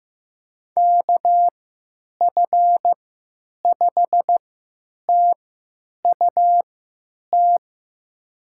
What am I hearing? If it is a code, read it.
Morse code, heard as a keyed tone, about 15 wpm, spaced as for 9 wpm, 700 Hz. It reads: KF5TUT
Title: USA Call Signs - 6 Characters - 3x Character Spacing - 15wpm